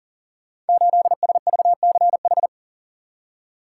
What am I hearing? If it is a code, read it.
Morse code, heard as a keyed tone, about 42 wpm, 690 Hz. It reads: 8SVCH
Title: Code Groups - 5 Letter:Numbers - 40wpm